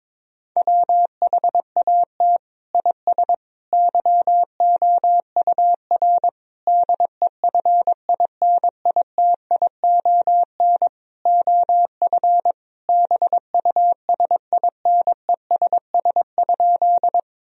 Morse code, heard as a keyed tone, about 22 wpm, 700 Hz. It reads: WHAT IS YOUR DEFINITION OF BUSINESS?
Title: Sentences from Top 1000 Words - Review Part 4 - 22wpm